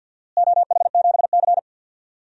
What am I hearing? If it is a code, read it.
Morse code, heard as a keyed tone, about 51 wpm, 700 Hz. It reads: OH7C
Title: DXCC-100 Call Signs - 4 Characters - Repeated 3x - 50wpm